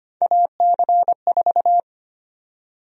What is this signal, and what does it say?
Morse code, heard as a keyed tone, about 25 wpm, 700 Hz. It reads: AC4
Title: USA Call Sign Prefixes - 25wpm